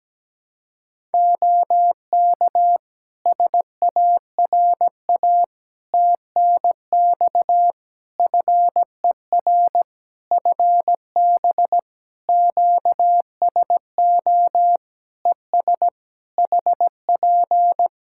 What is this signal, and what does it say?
Morse code, heard as a keyed tone, about 17 wpm, 700 Hz. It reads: OK SARA TNX FER FB QSO ES HP
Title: Practice QSO - Exchange 4 - Ending - 17wpm